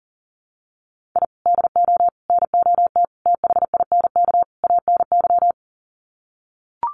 Morse code, heard as a keyed tone, about 40 wpm, 700 Hz. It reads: I DO NOT THINK ANY
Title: Sets of 5 Words Encore 40wpm